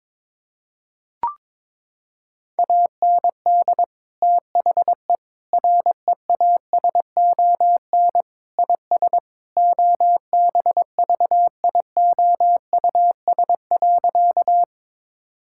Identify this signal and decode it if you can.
Morse code, heard as a keyed tone, about 21 wpm, 700 Hz. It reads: AND THE REASON IS OBVIOUS.
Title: Sentences from Top 1600 Words 22wpm